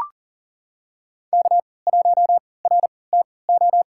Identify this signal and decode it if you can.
Morse code, heard as a keyed tone, about 40 wpm, spaced as for 23 wpm, 700 Hz. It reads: K1RTO
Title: USA Call Signs - 5 Characters - 3x Character Spacing - 40wpm